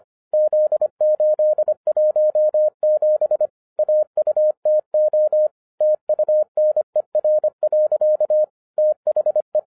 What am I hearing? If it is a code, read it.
Morse code, heard as a keyed tone, about 24 wpm, 610 Hz. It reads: Z817 AUTO TUNER. THE
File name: 2011-07-mp3 / n24 QRP OTAKU 25wpm